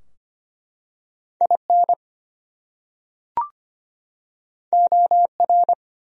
Morse code, heard as a keyed tone, about 25 wpm, 700 Hz. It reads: IN OR
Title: Top 100 Words 25wpm